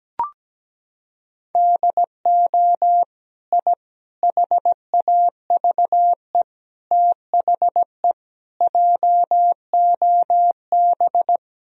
Morse code, heard as a keyed tone, about 17 wpm, 700 Hz. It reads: DO I HAVE THE JOB
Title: Sentences from Top 1000 Words - Review Part 3 - 17wpm